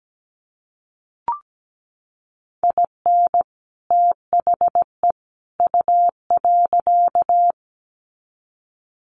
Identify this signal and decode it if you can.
Morse code, heard as a keyed tone, about 17 wpm, 700 Hz. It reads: IN THE U.
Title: Sets of 3 Words 17wpm